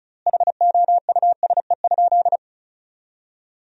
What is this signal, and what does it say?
Morse code, heard as a keyed tone, about 34 wpm, 700 Hz. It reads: HOUSE?
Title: 3x Word-Spacing - Sentences from Top 200 Words - 35wpm